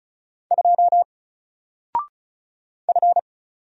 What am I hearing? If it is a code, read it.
Morse code, heard as a keyed tone, about 33 wpm, 700 Hz. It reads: 2 F
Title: Single Letter-Number 35wpm